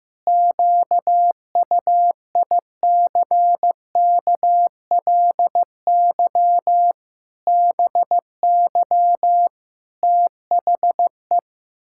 Morse code, heard as a keyed tone, about 16 wpm, 700 Hz. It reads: QUICKLY BY THE
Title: Pangram Challenge v2 - 15wpm